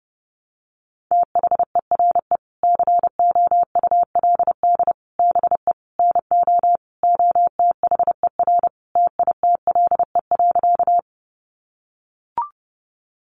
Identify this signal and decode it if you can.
Morse code, heard as a keyed tone, about 30 wpm, 700 Hz. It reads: THERE COULD BE NO OTHER TITLE.
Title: Sentences from Top 1200 Words 30wpm